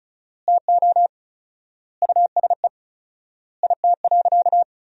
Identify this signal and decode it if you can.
Morse code, heard as a keyed tone, about 34 wpm, 700 Hz. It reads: TO USE IT.
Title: 4x Word-Spacing - Sentences from Top 500 Words - 35wpm